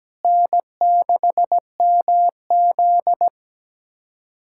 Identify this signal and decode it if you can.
Morse code, heard as a keyed tone, about 17 wpm, 700 Hz. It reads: N6MZ
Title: K1USN SST QSO Practice - 17wpm